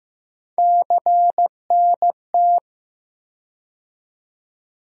Morse code, heard as a keyed tone, about 15 wpm, 700 Hz. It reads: CNT
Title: USA Call Sign Postfixes - 3 Letters 15wpm